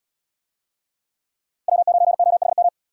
Morse code, heard as a keyed tone, about 63 wpm, 700 Hz. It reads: W9OHM